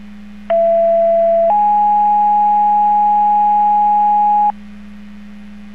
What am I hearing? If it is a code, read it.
Two-tone page, heard as a one-second tone, then a three-second tone, about 670 then 830 Hz, with noise. It is heard from 0.5 to 4.5 s.